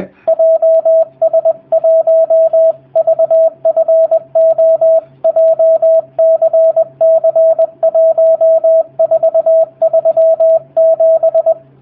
Morse code, heard as a keyed tone, about 20 wpm, 650 Hz, with radio noise. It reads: JS1VFO JCC1437
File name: jh8cbh / 2019cw / sp20- w